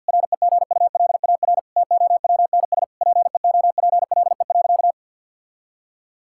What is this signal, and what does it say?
Morse code, heard as a keyed tone, about 50 wpm, 700 Hz. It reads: REGULAR TOWNS PEOPLE.
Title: Sentences from Top 1000 Words - Review Part 6 - 50wpm